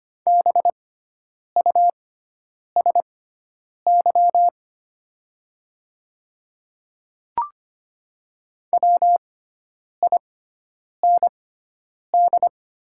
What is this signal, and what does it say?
Morse code, heard as a keyed tone, about 25 wpm, spaced as for 9 wpm, 700 Hz. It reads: BUSY WIND
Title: 6x Character Spacing - Top 500 Words - 25wpm